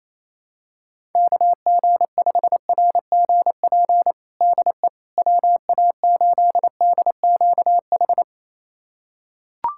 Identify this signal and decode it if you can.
Morse code, heard as a keyed tone, about 27 wpm, 700 Hz. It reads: KG5RGP DE WA8DQH
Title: USA Callsign DE Callsign 28wpm